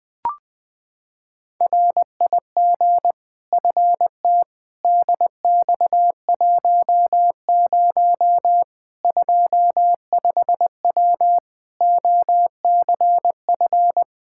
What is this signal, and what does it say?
Morse code, heard as a keyed tone, about 20 wpm, 700 Hz. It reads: RIG FT DX10 25W OCF
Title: Practice QSO - Exchange 2 - Info - 20wpm